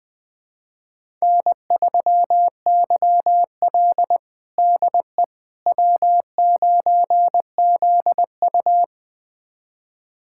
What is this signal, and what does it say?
Morse code, heard as a keyed tone, about 20 wpm, 700 Hz. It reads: N3YL DE W9ZU
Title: USA Callsign DE Callsign - 4 Characters - 20wpm